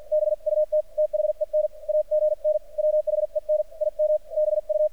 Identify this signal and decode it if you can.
Morse code, heard as a keyed tone, about 46 wpm, 610 Hz. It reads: BUT THEN AGN WHEN IM BU
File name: archived / CRSnet